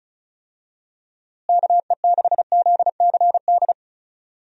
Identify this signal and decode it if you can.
Morse code, heard as a keyed tone, about 36 wpm, 700 Hz. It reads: KE6ZCD
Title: USA Call Signs - 6 Characters - Repeated 2x - 35wpm